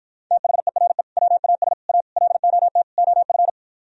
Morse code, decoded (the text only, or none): THERE WAS A LOT OF